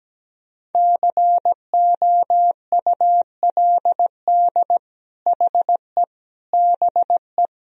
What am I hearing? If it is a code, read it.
Morse code, heard as a keyed tone, about 17 wpm, 700 Hz. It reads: COULD HE BE
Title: Sentences from Top 1000 Words - Review Part 4 - 17wpm